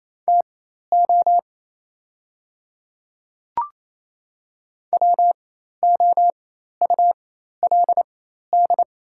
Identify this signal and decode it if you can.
Morse code, heard as a keyed tone, about 27 wpm, spaced as for 13 wpm, 690 Hz. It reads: TO WOULD
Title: 4x Character Spacing - Top 100 Words - 28wpm